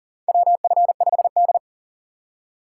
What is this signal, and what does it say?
Morse code, heard as a keyed tone, about 40 wpm, 700 Hz. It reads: WF5D
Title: USA Call Signs - 4 Characters - 4x Character Spacing - 40wpm